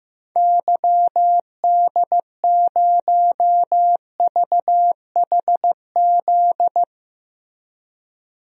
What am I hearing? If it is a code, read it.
Morse code, heard as a keyed tone, about 15 wpm, 700 Hz. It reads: YD0VHZ
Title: DXCC-100 Call Signs - 6 Characters - Repeated 2x - 15wpm